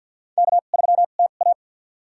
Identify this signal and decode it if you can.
Morse code, heard as a keyed tone, about 48 wpm, spaced as for 35 wpm, 690 Hz. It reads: K3TA
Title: USA Call Signs - 2x Character Spacing - 50wpm